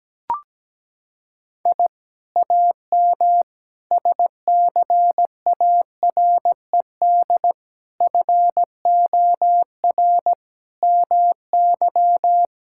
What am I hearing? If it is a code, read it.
Morse code, heard as a keyed tone, about 17 wpm, 700 Hz. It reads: I AM SCARED FOR MY
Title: Sentences from Top 1000 Words - Review Part 5 - 17wpm